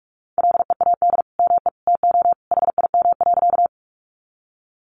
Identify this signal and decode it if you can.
Morse code, heard as a keyed tone, about 44 wpm, 700 Hz. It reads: LEAD ME TO HIM.
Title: Sentences from Top 500 Words 45wpm